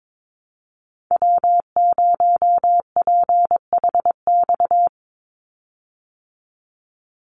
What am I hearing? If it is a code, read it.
Morse code, heard as a keyed tone, about 22 wpm, 700 Hz. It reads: W0PHX
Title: Call Signs Repeated 3x - 22wpm